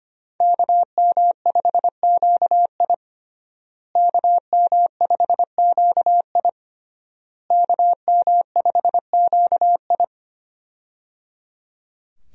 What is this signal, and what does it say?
Morse code, heard as a keyed tone, about 25 wpm, 700 Hz. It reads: KM5QI KM5QI KM5QI
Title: USA Call Signs - 5 Characters - Repeated 3x - 25wpm